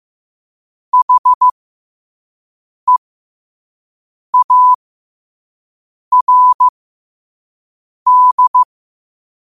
Morse code, heard as a keyed tone, about 15 wpm, spaced as for 5 wpm, 1000 Hz. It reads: HEARD
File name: qotdfiles / 5wpm